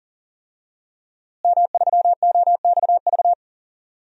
Morse code, heard as a keyed tone, about 40 wpm, 700 Hz. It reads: M3OXV
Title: DXCC-100 Call Signs - 5 Characters - Repeated 3x - 40wpm